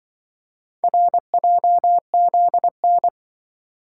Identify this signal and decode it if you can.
Morse code, heard as a keyed tone, about 24 wpm, 700 Hz. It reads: RJZN
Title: Speed-Racing - Four Letter-Number Combinations - 20wpm